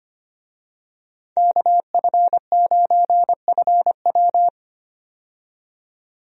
Morse code, heard as a keyed tone, about 25 wpm, 700 Hz. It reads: KF9FW
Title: USA Call Signs - 5 Characters - 25wpm